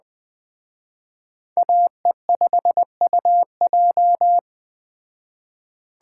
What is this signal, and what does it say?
Morse code, heard as a keyed tone, about 19 wpm, 700 Hz. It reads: AE5UJ